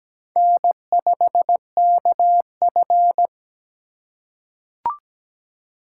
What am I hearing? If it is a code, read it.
Morse code, heard as a keyed tone, about 17 wpm, 700 Hz. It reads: N5KF
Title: USA Call Signs - 4 Characters - 8x Character Spacing - 17wpm